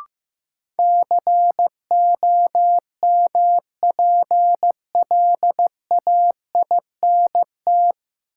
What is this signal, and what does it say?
Morse code, heard as a keyed tone, about 15 wpm, 700 Hz. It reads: COMPLAINT
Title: Sentences from Top 1400 Words - New Word Drill Alt - 15wpm